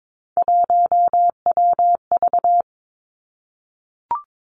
Morse code, heard as a keyed tone, about 22 wpm, 700 Hz. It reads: 1WV
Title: Three Challenging Letter-Number Combinations 22wpm